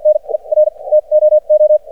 Morse code, heard as a keyed tone, about 55 wpm, spaced as for 38 wpm, 610 Hz. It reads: NER TOO
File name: archived / CRSnet